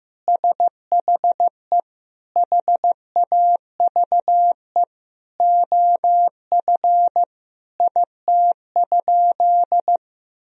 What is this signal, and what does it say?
Morse code, heard as a keyed tone, about 16 wpm, 700 Hz. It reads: SHE HAVE OF IT?
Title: Sentences from Top 1000 Words - Review Part 3 - 15wpm